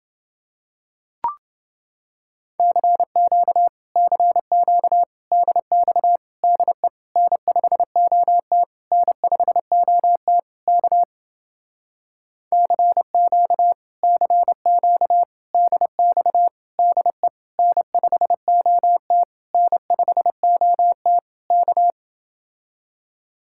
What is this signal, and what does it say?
Morse code, heard as a keyed tone, about 29 wpm, 700 Hz. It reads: CQ CQ DX DE N5OT N5OT K CQ CQ DX DE N5OT N5OT K